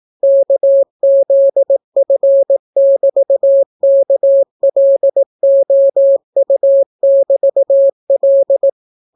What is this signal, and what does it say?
Morse code, heard as a keyed tone, about 18 wpm, 550 Hz. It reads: KZF<BT>KLOU<BT>L